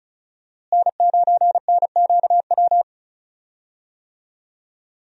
Morse code, heard as a keyed tone, about 36 wpm, 700 Hz. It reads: N9NQW
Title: USA Call Signs - 5 Characters - Repeated 3x - 35wpm